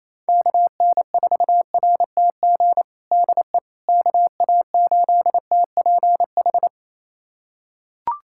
Morse code, heard as a keyed tone, about 28 wpm, 700 Hz. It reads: KN4RTG DE KA8TPH
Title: USA Callsign DE Callsign - 6 Characters - 28wpm